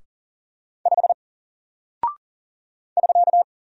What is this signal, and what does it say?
Morse code, heard as a keyed tone, about 40 wpm, 700 Hz. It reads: 5 <SK>
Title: Single Character (Letter:Number:Punctuation:Prosign) Alt 40wpm